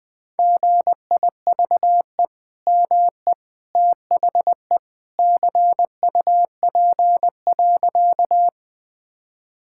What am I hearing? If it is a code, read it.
Morse code, heard as a keyed tone, about 20 wpm, 700 Hz. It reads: GIVE ME THE CUP.
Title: Sentences from Top 1600 Words 20wpm